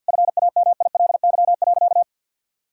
Morse code, heard as a keyed tone, about 50 wpm, 700 Hz. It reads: FAMILY.
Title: Sentences from Top 1000 Words - Review Part 6 - 50wpm